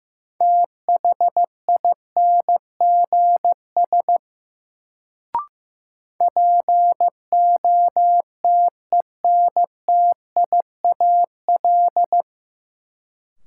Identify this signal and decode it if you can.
Morse code, heard as a keyed tone, about 15 wpm, 700 Hz. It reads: THINGS POTENTIAL
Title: Sentences from Top 1000 Words - Review Part 4 - New Word Drill - 15wpm